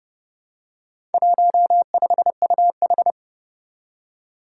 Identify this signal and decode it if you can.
Morse code, heard as a keyed tone, about 30 wpm, 700 Hz. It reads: 15UH
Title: Four Challenging Letter-Number Combinations 30wpm